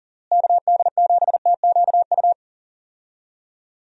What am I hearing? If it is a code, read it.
Morse code, heard as a keyed tone, about 40 wpm, 700 Hz. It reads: KD7TQU
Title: USA Call Signs - 6 Characters - Repeated 3x - 40wpm